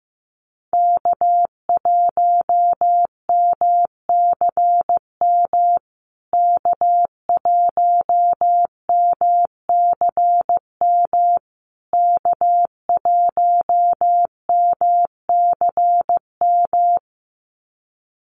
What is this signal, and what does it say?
Morse code, heard as a keyed tone, about 15 wpm, 700 Hz. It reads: K1MCM K1MCM K1MCM
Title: USA Call Signs Repeated 3x 15wpm